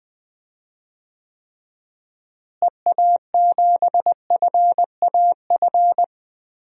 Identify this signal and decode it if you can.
Morse code, heard as a keyed tone, about 20 wpm, 700 Hz. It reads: EA7FAF
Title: DXCC-100 Call Signs - 6 Characters - Repeated 3x - 20wpm